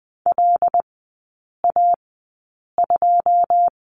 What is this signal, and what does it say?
Morse code, heard as a keyed tone, about 20 wpm, spaced as for 8 wpm, 690 Hz. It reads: LA2
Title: 081 Lesson 14e - Number 2 - Three Characters 20wpm